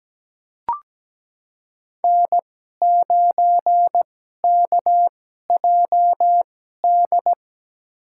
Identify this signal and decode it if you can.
Morse code, heard as a keyed tone, about 17 wpm, spaced as for 12 wpm, 700 Hz. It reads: N9KJD
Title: USA Call Signs - 5 Characters - 2x Character Spacing - 17wpm